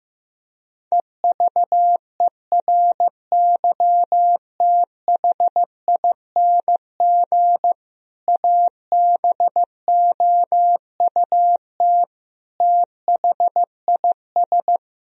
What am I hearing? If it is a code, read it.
Morse code, heard as a keyed tone, about 15 wpm, 700 Hz. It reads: EVERYTHING ABOUT THIS
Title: Sentences from Top 1000 Words - Review Part 1 - 15wpm